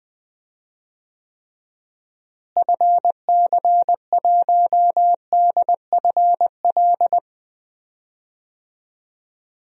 Morse code, heard as a keyed tone, about 20 wpm, 700 Hz. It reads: FC1DFL
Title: DXCC-100 Call Signs - 6 Characters - Repeated 2x - 20wpm